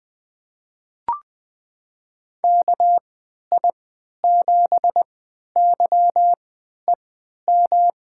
Morse code, heard as a keyed tone, about 20 wpm, spaced as for 11 wpm, 700 Hz. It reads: KI7YEM
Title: USA Call Signs - 6 Characters - 3x Character Spacing - 20wpm